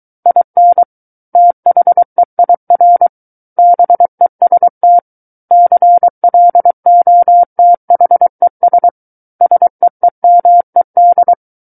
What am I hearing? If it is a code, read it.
Morse code, heard as a keyed tone, about 23 wpm, 700 Hz. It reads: IN THEIR BEST CLOTHES SEEMED
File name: Book One / 14-Chapter 14 - 23 WPM